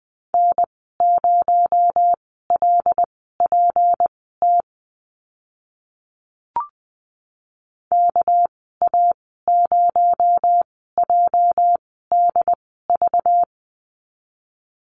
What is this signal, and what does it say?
Morse code, heard as a keyed tone, about 20 wpm, spaced as for 14 wpm, 700 Hz. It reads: N0LPT KA0JDV
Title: 245 Lesson 39g - Letter z - Call Signs 20wpm